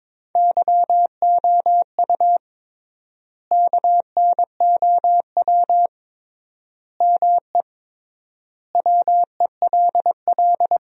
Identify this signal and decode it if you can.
Morse code, heard as a keyed tone, about 22 wpm, 700 Hz. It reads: YOU KNOW ME WELL
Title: 3x Word-Spacing - Sentences from Top 400 Words - 22wpm